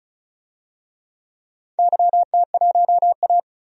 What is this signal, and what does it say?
Morse code, heard as a keyed tone, about 36 wpm, 700 Hz. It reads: YT1A
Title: DXCC-100 Call Signs - 4 Characters - 35wpm